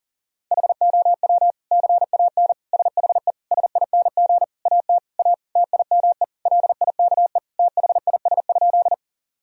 Morse code, heard as a keyed tone, about 37 wpm, 700 Hz. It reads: HOW CAN SHE SING AT A TIME LIKE THIS?
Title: Sentences from Top 1000 Words - Review Part 2 - 40wpm